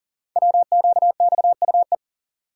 Morse code, heard as a keyed tone, about 40 wpm, 700 Hz. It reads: WQXUE
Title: Code Groups - 5 Letters - 40wpm